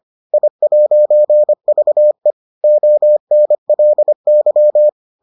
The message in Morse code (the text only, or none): I'VE ONLY